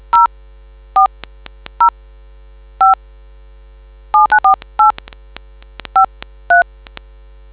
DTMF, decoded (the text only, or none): 0405794853